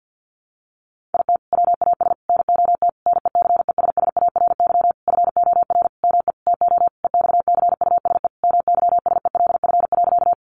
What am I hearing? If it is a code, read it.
Morse code, heard as a keyed tone, about 50 wpm, 700 Hz. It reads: IT WAS NOT NECESSARY FOR ME TO EXCUSE MYSELF.
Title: Sentences from Top 1300 Words 50wpm